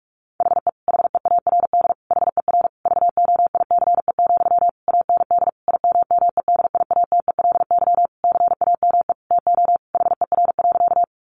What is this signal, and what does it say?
Morse code, heard as a keyed tone, about 44 wpm, 700 Hz. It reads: HE HEARD HER VOICE, AND IMMEDIATELY CAME TO HER.
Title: Sentences from Top 1100 Words 45wpm